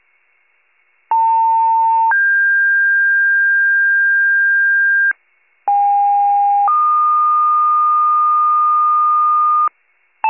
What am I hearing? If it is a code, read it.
Two-tone page, heard as a one-second tone, then a three-second tone, about 900 then 1600 Hz, about 810 then 1220 Hz, with noise. It is heard from 1.1 to 9.7 s.
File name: wcalert